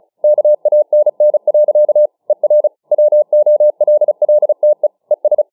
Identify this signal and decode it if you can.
Morse code, heard as a keyed tone, about 37 wpm, 600 Hz. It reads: KANN. ER WOLLTE ES